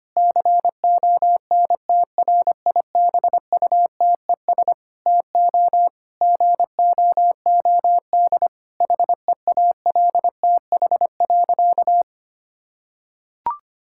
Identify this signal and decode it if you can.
Morse code, heard as a keyed tone, about 25 wpm, 700 Hz. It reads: CONTRIBUTES TO GOOD HEALTH.
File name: practice / 25wpm / Difficult Sentences with Hints - 25wpm